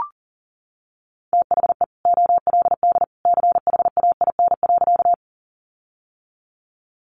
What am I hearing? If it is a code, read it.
Morse code, heard as a keyed tone, about 40 wpm, 700 Hz. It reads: THE OLD CHAIN.
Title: Sentences from Top 1500 Words 40wpm